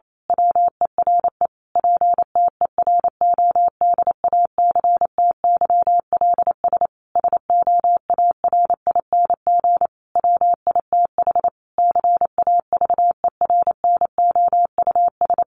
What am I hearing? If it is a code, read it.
Morse code, heard as a keyed tone, about 28 wpm, 700 Hz. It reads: WERE PTERODACTYLS SOARING WITH CAVERNOUS
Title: A Sound Of Thunder 28wpm